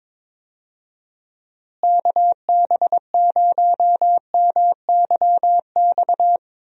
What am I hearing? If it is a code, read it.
Morse code, heard as a keyed tone, about 22 wpm, 700 Hz. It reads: KB0MYX
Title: USA Call Signs - 6 Characters - Repeated 3x - 22wpm